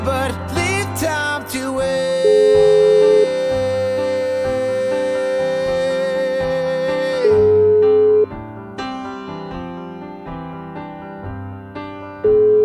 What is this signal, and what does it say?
Ringback tone (Europe, 425 Hz).